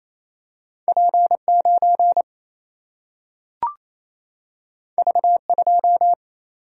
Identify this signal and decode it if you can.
Morse code, heard as a keyed tone, about 28 wpm, 700 Hz. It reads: P9 V2
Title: Single-Letter Single-Number 28wpm